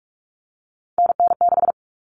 Morse code, heard as a keyed tone, about 44 wpm, 700 Hz. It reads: NN6
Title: Common Cut Numbers - 3-Digit 45wpm